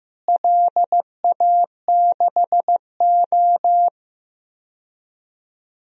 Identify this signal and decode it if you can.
Morse code, heard as a keyed tone, about 15 wpm, 700 Hz. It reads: LA6O
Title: DXCC-100 Call Signs - 4 Characters - Repeated 3x - 15wpm